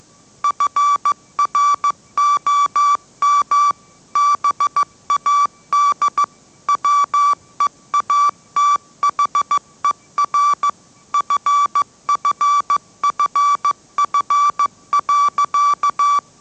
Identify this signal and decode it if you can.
Morse code, heard as a keyed tone, about 15 wpm, 1160 Hz. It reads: FROM BAD WEATHER FFFF.
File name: mrx / mrx065 16wpm